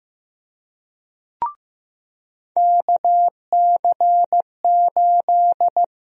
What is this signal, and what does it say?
Morse code, heard as a keyed tone, about 15 wpm, 700 Hz. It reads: KC8
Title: USA Call Sign Prefixes - 15wpm